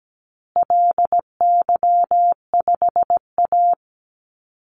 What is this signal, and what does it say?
Morse code, heard as a keyed tone, about 17 wpm, 700 Hz. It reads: LY5A
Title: Call Signs Repeated 2x - 17wpm